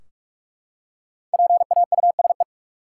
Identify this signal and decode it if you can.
Morse code, heard as a keyed tone, about 46 wpm, 700 Hz. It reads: PAUSE
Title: Five Letter Words - 45wpm